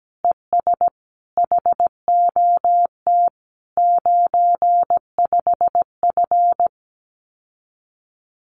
Morse code, heard as a keyed tone, about 17 wpm, 700 Hz. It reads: ES HOT 95F
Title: Common QSO Phrases Encore 17wpm